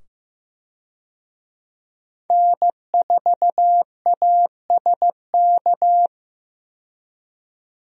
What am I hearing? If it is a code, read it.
Morse code, heard as a keyed tone, about 15 wpm, 700 Hz. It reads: N4ASK
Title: USA Call Signs - 5 Characters - 15wpm